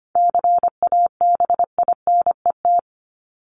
Morse code, heard as a keyed tone, about 25 wpm, 700 Hz. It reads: CABINET
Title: Top 1751-2000 Words 25wpm